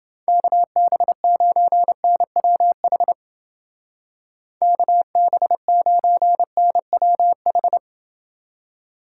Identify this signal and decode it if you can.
Morse code, heard as a keyed tone, about 30 wpm, 700 Hz. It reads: KB9NWH KB9NWH